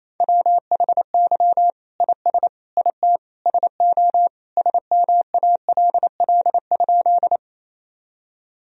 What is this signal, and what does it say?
Morse code, heard as a keyed tone, about 27 wpm, 700 Hz. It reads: WHY IS IT SO SMALL?
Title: Sentences from Top 1000 Words - Review Part 1 - 28wpm